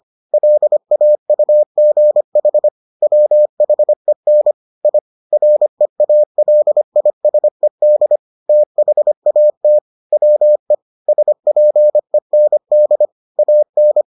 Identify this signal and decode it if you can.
Morse code, heard as a keyed tone, about 25 wpm, 600 Hz. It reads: LAUGH WHEN I REALISED THAT WE SPEND AN